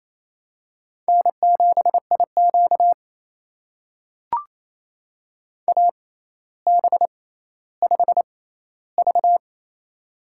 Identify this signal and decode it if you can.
Morse code, heard as a keyed tone, about 28 wpm, spaced as for 14 wpm, 700 Hz. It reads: N7IQ AB5V
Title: USA Call Signs - 4 Characters - 6x Character Spacing - 28wpm